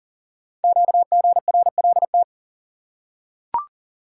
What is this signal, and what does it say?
Morse code, heard as a keyed tone, about 38 wpm, 700 Hz. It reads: QGRLT
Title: Code Groups - 5 Letters - 40wpm